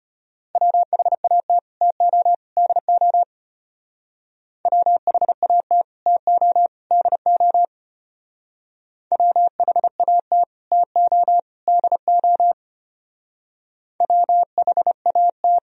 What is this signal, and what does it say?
Morse code, heard as a keyed tone, about 32 wpm, 700 Hz. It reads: WHAT TO DO WHAT TO DO WHAT TO DO WHAT